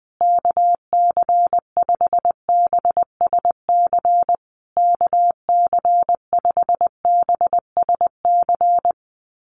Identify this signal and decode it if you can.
Morse code, heard as a keyed tone, about 20 wpm, 700 Hz. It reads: KC5BSC KC5BSC
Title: USA Call Signs Repeated 3x 20wpm